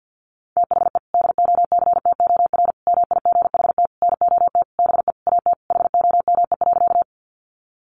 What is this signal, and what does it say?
Morse code, heard as a keyed tone, about 50 wpm, 700 Hz. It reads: THE DOCTOR MIGHT NOT BE AT HOME.
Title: Sentences from Top 1500 Words 50wpm